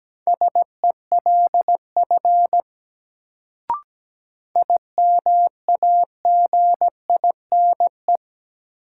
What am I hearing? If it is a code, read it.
Morse code, heard as a keyed tone, about 17 wpm, 700 Hz. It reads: SELF IMAGINE
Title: Sentences from Top 1400 Words - New Word Drill Alt - 17wpm